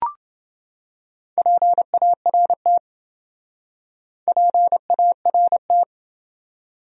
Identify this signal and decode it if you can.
Morse code, heard as a keyed tone, about 28 wpm, 700 Hz. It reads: PART PART